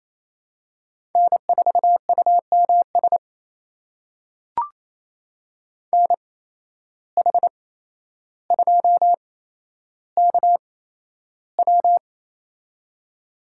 Morse code, heard as a keyed tone, about 28 wpm, spaced as for 12 wpm, 700 Hz. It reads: N4UMS NH2KW
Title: USA Call Signs - 5 Characters - 8x Character Spacing - 28wpm